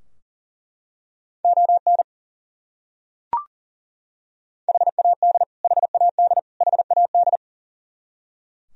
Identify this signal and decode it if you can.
Morse code, heard as a keyed tone, about 40 wpm, 700 Hz. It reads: ON HAD HAD HAD
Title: Top 100 Words Repeated 40wpm